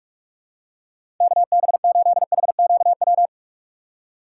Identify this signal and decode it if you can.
Morse code, heard as a keyed tone, about 41 wpm, 700 Hz. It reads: KB8HQW
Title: USA Call Signs - 6 Characters - 2x Character Spacing - 45wpm